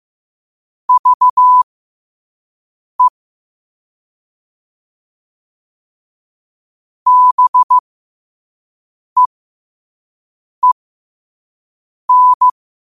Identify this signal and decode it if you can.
Morse code, heard as a keyed tone, about 15 wpm, spaced as for 5 wpm, 1000 Hz. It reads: VE BEEN
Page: New quotes every day in morse code at 5 Words per minute.